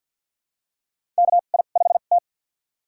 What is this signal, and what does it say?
Morse code, heard as a keyed tone, about 51 wpm, spaced as for 36 wpm, 700 Hz. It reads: KI5T
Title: USA Call Signs - 2x Character Spacing - 50wpm